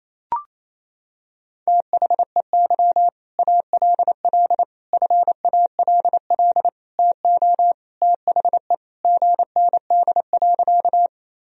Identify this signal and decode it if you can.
Morse code, heard as a keyed tone, about 28 wpm, 700 Hz. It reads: THEY ALL FALL TO THE GND.
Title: Sentences from Top 500 Words and Abbreviations 28wpm